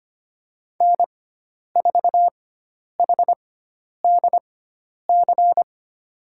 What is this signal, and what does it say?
Morse code, heard as a keyed tone, about 25 wpm, spaced as for 10 wpm, 700 Hz. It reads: N4HDC
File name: Lesson 16g - Letter c - Call Signs 25wpm